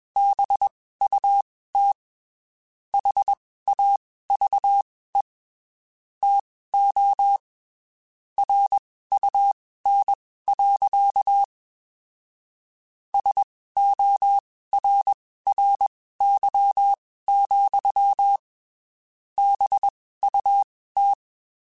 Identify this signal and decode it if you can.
Morse code, heard as a keyed tone, about 21 wpm, spaced as for 15 wpm, 800 Hz. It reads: BUT HAVE TO RUN. SORRY, BUT
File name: (spaced) / QSOs Spaced-mp3s / kc7haa @20 sp